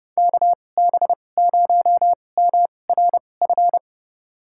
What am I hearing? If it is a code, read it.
Morse code, heard as a keyed tone, about 31 wpm, spaced as for 22 wpm, 700 Hz. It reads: KB0MRF
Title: USA Call Signs - 6 Characters - 2x Character Spacing - 30wpm